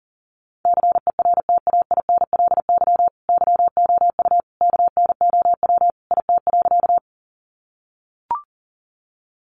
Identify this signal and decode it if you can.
Morse code, heard as a keyed tone, about 40 wpm, 700 Hz. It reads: CERTAINLY YOU KNOW IT.
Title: Sentences from Top 900 Words 40wpm